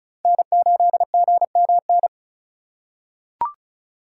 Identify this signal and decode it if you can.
Morse code, heard as a keyed tone, about 36 wpm, 700 Hz. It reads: N8GMN
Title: USA Call Signs - 5 Characters - 2x Character Spacing - 35wpm